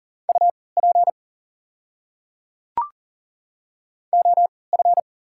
Morse code, heard as a keyed tone, about 39 wpm, spaced as for 22 wpm, 690 Hz. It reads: UP OF
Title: 3x Character Spacing - Top 100 Words - 40wpm